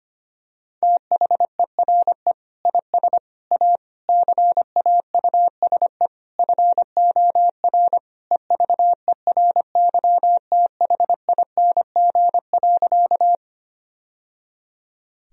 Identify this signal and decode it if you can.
Morse code, heard as a keyed tone, about 25 wpm, 700 Hz. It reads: THERE IS A CAUSE FOR EVERYTHING.
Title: Sentences from Top 1000 Words - Review Part 1 - 25wpm